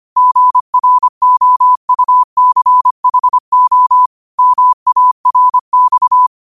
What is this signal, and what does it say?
Morse code, heard as a keyed tone, about 25 wpm, 1000 Hz. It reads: GROUCHO MARX